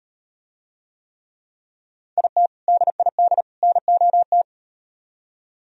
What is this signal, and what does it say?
Morse code, heard as a keyed tone, about 38 wpm, 700 Hz. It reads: IT DID NOT